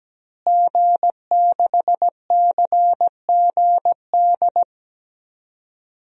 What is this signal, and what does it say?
Morse code, heard as a keyed tone, about 17 wpm, 700 Hz. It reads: G6CGD
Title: DXCC-100 Call Signs - 5 Characters - Repeated 2x - 17wpm